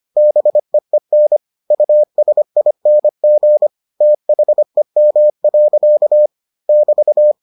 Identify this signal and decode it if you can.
Morse code, heard as a keyed tone, about 25 wpm, 600 Hz. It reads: BEEN USING THEM. <BT>